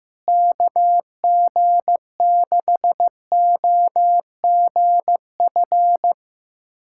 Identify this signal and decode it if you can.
Morse code, heard as a keyed tone, about 15 wpm, 700 Hz. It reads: KG6OGF
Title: USA Callsign DE Callsign - 6 Characters - 15wpm